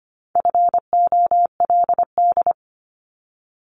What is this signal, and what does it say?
Morse code, heard as a keyed tone, about 25 wpm, 700 Hz. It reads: FOLD
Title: Multi-letter Phonemes Repeated 3x 25wpm